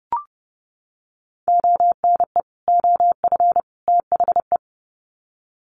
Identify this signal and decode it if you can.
Morse code, heard as a keyed tone, about 30 wpm, 700 Hz. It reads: ONE OF THE
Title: Sets of 3 Words 30wpm